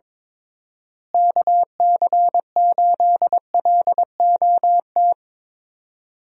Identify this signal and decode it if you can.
Morse code, heard as a keyed tone, about 22 wpm, 700 Hz. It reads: KC8LOT
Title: USA Call Signs - 6 Characters - Repeated 2x - 22wpm